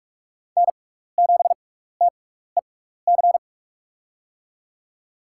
Morse code, heard as a keyed tone, about 44 wpm, spaced as for 15 wpm, 700 Hz. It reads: N7TEC
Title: USA Call Signs - 6x Character Spacing - 45wpm